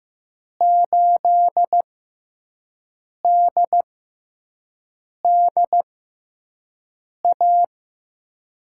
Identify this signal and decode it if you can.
Morse code, heard as a keyed tone, about 15 wpm, spaced as for 5 wpm, 700 Hz. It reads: 8DDA
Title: USA Call Signs - 6 Characters - 6x Character Spacing - 15wpm